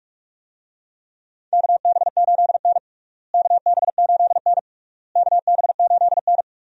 Morse code, heard as a keyed tone, about 41 wpm, 700 Hz. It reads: KB8N KB8N KB8N
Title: USA Call Signs - 4 Characters - Repeated 3x - 45wpm